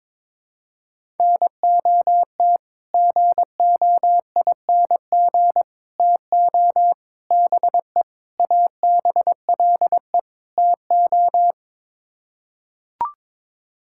Morse code, heard as a keyed tone, about 22 wpm, 700 Hz. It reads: NOT GOING TO BE ABLE TO
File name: practice / 22wpm / Sets of 6 Words - 22wpm